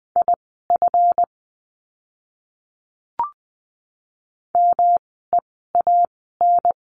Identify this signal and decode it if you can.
Morse code, heard as a keyed tone, about 20 wpm, spaced as for 14 wpm, 700 Hz. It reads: IF MEAN
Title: 230 Lesson 37f - Letter x - Words 20wpm